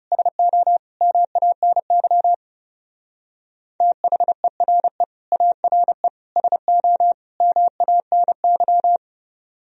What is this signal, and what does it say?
Morse code, heard as a keyed tone, about 38 wpm, 700 Hz. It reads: SO MANY THERE ARE SO MANY